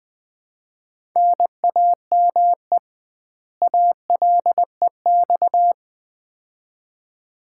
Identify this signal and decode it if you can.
Morse code, heard as a keyed tone, about 20 wpm, 700 Hz. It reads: NAME ALEX
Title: Common QSO Phrases - 2x Word-Spacing - 20wpm